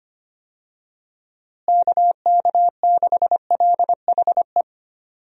Morse code, heard as a keyed tone, about 25 wpm, 700 Hz. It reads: KK6LHE
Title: USA Call Signs - 6 Characters - Repeated 2x - 25wpm